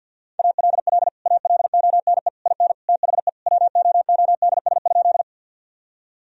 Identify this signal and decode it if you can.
Morse code, heard as a keyed tone, about 50 wpm, 700 Hz. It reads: ALL ALONE IN THE WOODS?
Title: Sentences from Top 1000 Words - Review Part 2 - 50wpm